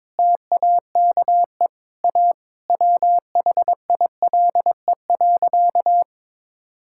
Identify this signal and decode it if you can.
Morse code, heard as a keyed tone, about 22 wpm, 700 Hz. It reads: TAKE A WHILE.
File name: practice / 22wpm / Sentences with 4-Word Phrases - 22wpm